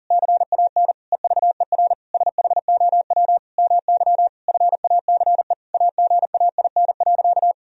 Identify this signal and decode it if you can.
Morse code, heard as a keyed tone, about 40 wpm, 700 Hz. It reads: CAN EVER SHOW MY FACE AGAIN.
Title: Sentences from Top 1000 Words - Review Part 2 - 40wpm